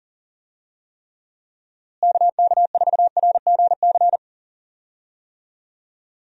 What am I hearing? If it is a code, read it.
Morse code, heard as a keyed tone, about 37 wpm, 700 Hz. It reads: KK4RGC
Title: USA Call Signs - 6 Characters - Repeated 3x - 40wpm